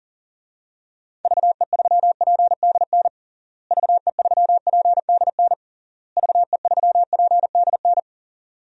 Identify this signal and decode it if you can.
Morse code, heard as a keyed tone, about 40 wpm, 700 Hz. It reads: VE3PDN VE3PDN VE3PDN
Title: DXCC-100 Call Signs - 6 Characters - Repeated 3x - 40wpm